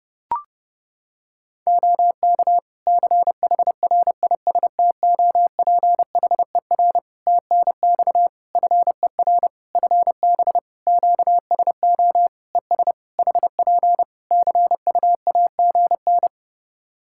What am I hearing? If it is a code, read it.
Morse code, heard as a keyed tone, about 28 wpm, 700 Hz. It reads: OK CHRISTOPHER TNX FER FB QSO ES HP CUAGN
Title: Practice QSO - Exchange 4 - Ending - 30wpm